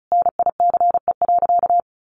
Morse code, heard as a keyed tone, about 34 wpm, 700 Hz. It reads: NICE.
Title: Sentences from Top 1100 Words 35wpm